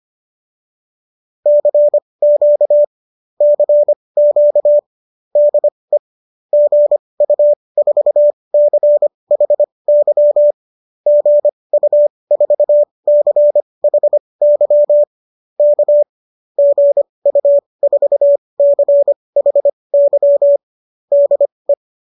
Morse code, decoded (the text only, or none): CQ CQ DE GU4CHY GU4CHY K GU4CHY DE